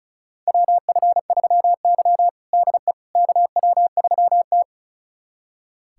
Morse code, heard as a keyed tone, about 36 wpm, 700 Hz. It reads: WF3Y DE KW3T
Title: USA Callsign DE Callsign - 4 Characters - 35wpm